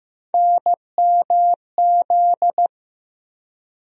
Morse code, heard as a keyed tone, about 15 wpm, 700 Hz. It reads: NMZ
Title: USA Call Sign Postfixes - 3 Letters 15wpm